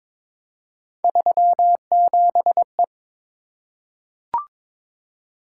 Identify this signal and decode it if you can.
Morse code, heard as a keyed tone, about 22 wpm, 700 Hz. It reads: 37E
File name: practice / 22wpm / Three Letter-Number Combinations - 22wpm